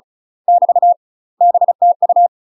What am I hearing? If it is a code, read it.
Morse code, heard as a keyed tone, about 34 wpm, 700 Hz. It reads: <BT> BTU